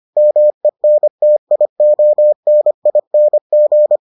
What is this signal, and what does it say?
Morse code, heard as a keyed tone, about 25 wpm, 600 Hz. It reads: MENTIONING